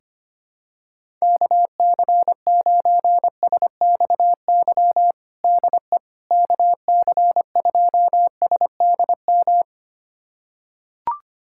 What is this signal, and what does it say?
Morse code, heard as a keyed tone, about 25 wpm, 700 Hz. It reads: KC9SXY DE KC2SDM
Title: USA Callsign DE Callsign - 6 Characters - 25wpm